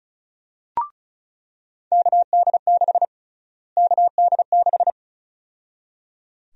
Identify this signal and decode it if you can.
Morse code, heard as a keyed tone, about 36 wpm, 700 Hz. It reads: KD6 KD6
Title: USA Call Sign Prefixes - Repeated 2x - 35wpm